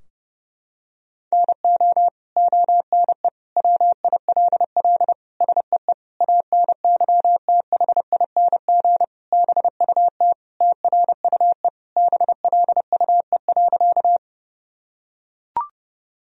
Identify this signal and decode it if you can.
Morse code, heard as a keyed tone, about 32 wpm, 700 Hz. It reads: NO ONE WILL SEE ANYTHING BUT TRUE BLUE.
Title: Sentences from Top 1000 Words - Review Part 3 - 30wpm